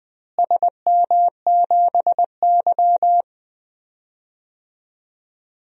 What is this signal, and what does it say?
Morse code, heard as a keyed tone, about 20 wpm, 700 Hz. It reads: SM7Y
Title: DXCC-100 Call Signs - 4 Characters - Repeated 3x - 20wpm